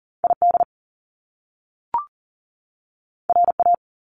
Morse code, heard as a keyed tone, about 40 wpm, 700 Hz. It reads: ID RA
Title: Top Two Letter Combinations in Words 40wpm